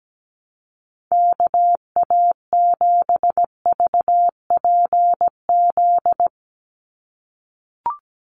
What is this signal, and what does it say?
Morse code, heard as a keyed tone, about 17 wpm, 700 Hz. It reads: KA7VPZ
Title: USA Call Signs 17wpm